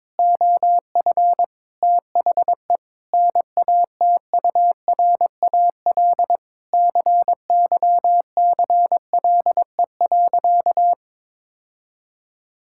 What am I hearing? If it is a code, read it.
Morse code, heard as a keyed tone, about 22 wpm, 700 Hz. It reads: OF THE NATURAL CYCLE.
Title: Sentences from Top 1000 Words - Review Part 6 - 22wpm